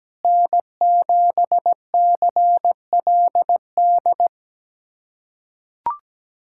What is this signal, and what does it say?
Morse code, heard as a keyed tone, about 17 wpm, 700 Hz. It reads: N7CLD
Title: USA Callsign DE Callsign - 5 Characters - 17wpm